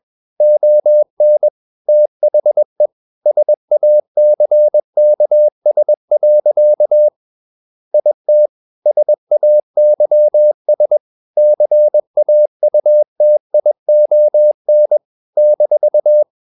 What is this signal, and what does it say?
Morse code, heard as a keyed tone, about 21 wpm, 600 Hz. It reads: ON THE SACKS. IT SAYS CAUTION -
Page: CW med Gnister